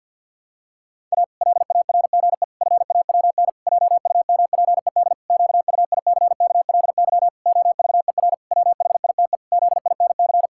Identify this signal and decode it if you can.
Morse code, heard as a keyed tone, about 45 wpm, 700 Hz. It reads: A LARGE FAWN JUMPED QUICKLY OVER WHITE ZINC
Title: Pangram Challenge v2 - 50wpm